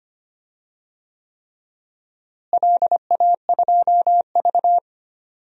Morse code, heard as a keyed tone, about 25 wpm, 700 Hz. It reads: LA2V
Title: DXCC-100 Call Signs - 4 Characters - Repeated 3x - 25wpm